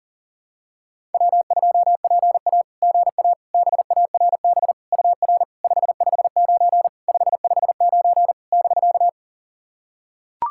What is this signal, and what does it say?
Morse code, heard as a keyed tone, about 40 wpm, 700 Hz. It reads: W2PA GA BARB UR 559 559 <BK>
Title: SOTA QSO Practice 40wpm